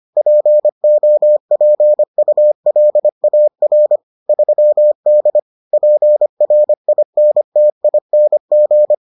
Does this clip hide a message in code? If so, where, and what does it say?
Yes: Morse code from 0.2 to 8.9 s, heard as a keyed tone, about 25 wpm, 600 Hz. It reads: POPULAR 3D PRINTING